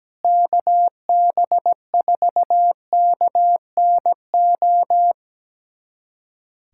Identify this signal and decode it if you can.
Morse code, heard as a keyed tone, about 17 wpm, 700 Hz. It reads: KB4KNO
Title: USA Callsign DE Callsign - 6 Characters - 17wpm